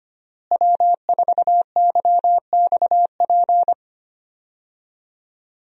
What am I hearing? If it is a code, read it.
Morse code, heard as a keyed tone, about 25 wpm, 700 Hz. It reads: W4YXP
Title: USA Call Signs - 5 Characters - Repeated 3x - 25wpm